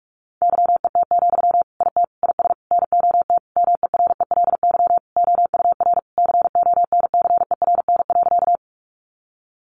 Morse code, heard as a keyed tone, about 44 wpm, 700 Hz. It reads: YET, IT IS NOT MERELY OUR CONCERN.
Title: Sentences from Top 1400 Words 45wpm